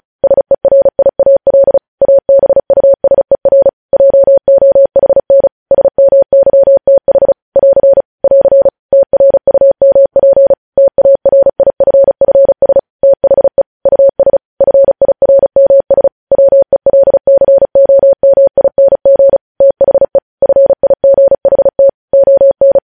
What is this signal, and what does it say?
Morse code, heard as a keyed tone, about 36 wpm, 550 Hz. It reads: SERIAL ABUSER JOHN SMYTH <AR> <AR> TRUMP TARIFFS THE US FIRMS WELCOMING THE FIGHT ON
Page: News headlines in Morse code at 35 wpm